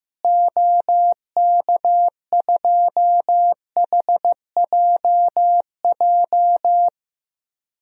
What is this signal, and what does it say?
Morse code, heard as a keyed tone, about 15 wpm, 700 Hz. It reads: OK2HJJ
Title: DXCC-100 Call Signs - 6 Characters - Repeated 2x - 15wpm